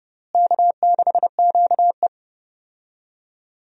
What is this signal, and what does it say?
Morse code, heard as a keyed tone, about 30 wpm, 700 Hz. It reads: K6QE
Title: USA Call Signs - 4 Characters - Repeated 3x - 30wpm